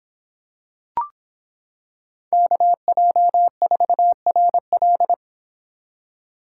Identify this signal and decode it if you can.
Morse code, heard as a keyed tone, about 26 wpm, 700 Hz. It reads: KJ4RL